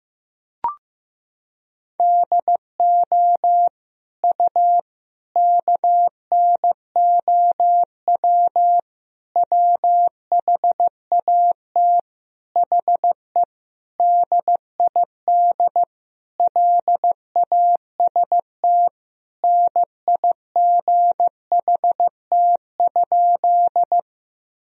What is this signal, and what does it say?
Morse code, heard as a keyed tone, about 15 wpm, 700 Hz. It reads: DO U KNOW WHAT HE DID LAST NIGHT?
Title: Sentences from Top 300 Words and Abbreviations 15wpm